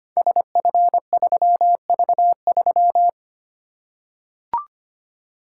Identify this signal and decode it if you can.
Morse code, heard as a keyed tone, about 25 wpm, 700 Hz. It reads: SF3V3
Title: Code Groups - 5 Letter:Numbers - 25wpm